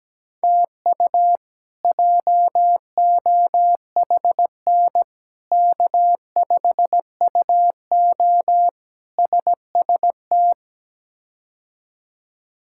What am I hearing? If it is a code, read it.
Morse code, heard as a keyed tone, about 17 wpm, 700 Hz. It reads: TU JOHN K5UO SST
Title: K1USN SST QSO Practice - 17wpm